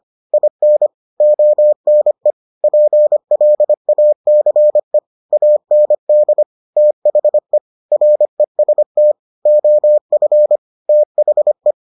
Morse code, heard as a keyed tone, about 25 wpm, 600 Hz. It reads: IN ONE PLACE AND THE REST OF THE